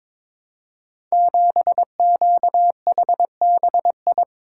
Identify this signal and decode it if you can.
Morse code, heard as a keyed tone, about 22 wpm, 700 Hz. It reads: 7QHBI
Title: Code Groups - 5 Letter:Numbers - 22wpm